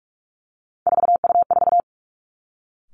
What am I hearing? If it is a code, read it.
Morse code, heard as a keyed tone, about 46 wpm, 700 Hz. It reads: 4U4
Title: Three Challenging Letter-Number Combinations 45wpm